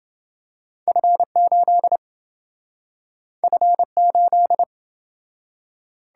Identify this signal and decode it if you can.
Morse code, heard as a keyed tone, about 26 wpm, 700 Hz. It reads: F8 F8